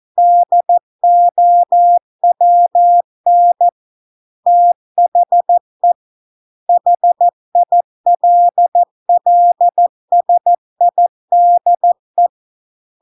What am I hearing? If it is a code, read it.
Morse code, heard as a keyed tone, about 14 wpm, 700 Hz. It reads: DOWN THE HILLSIDE